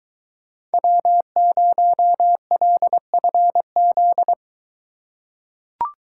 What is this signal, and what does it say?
Morse code, heard as a keyed tone, about 23 wpm, 700 Hz. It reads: W0LFZ